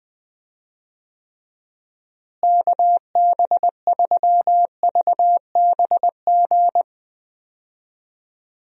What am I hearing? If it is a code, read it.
Morse code, heard as a keyed tone, about 20 wpm, 700 Hz. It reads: KB3VBG
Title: USA Call Signs - 6 Characters - Repeated 3x - 20wpm